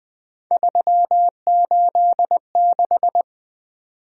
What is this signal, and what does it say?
Morse code, heard as a keyed tone, about 20 wpm, 700 Hz. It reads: 386
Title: Three Letter-Number Combinations - 20wpm